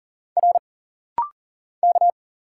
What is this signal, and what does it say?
Morse code, heard as a keyed tone, about 40 wpm, 700 Hz. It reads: R K
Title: Single Letters - Rapid-Fire 40wpm